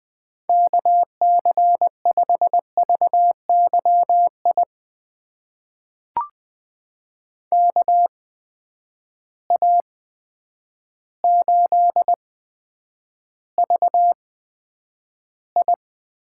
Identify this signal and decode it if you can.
Morse code, heard as a keyed tone, about 20 wpm, spaced as for 9 wpm, 700 Hz. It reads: KC5VYI KA8VI
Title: USA Call Signs - 6 Characters - 8x Character Spacing - 20wpm